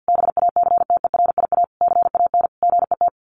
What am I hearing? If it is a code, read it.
Morse code, heard as a keyed tone, about 50 wpm, 700 Hz. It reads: BACTERIA CAN GET
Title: A Sound Of Thunder 50wpm